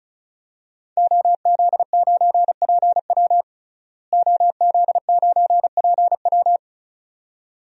Morse code, heard as a keyed tone, about 36 wpm, 700 Hz. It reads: OZ9PW OZ9PW
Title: DXCC-100 Call Signs - 5 Characters - Repeated 2x - 35wpm